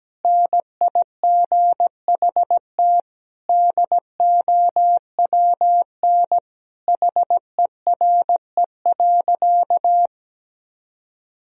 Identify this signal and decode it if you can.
Morse code, heard as a keyed tone, about 17 wpm, 700 Hz. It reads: NIGHT DOWN HERE.
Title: Sentences from Top 1000 Words - Review Part 2 - 17wpm